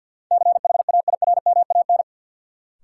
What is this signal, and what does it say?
Morse code, heard as a keyed tone, about 52 wpm, 700 Hz. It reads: CHAIRMAN
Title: Sets of 3 Words - New Word Drill - 50wpm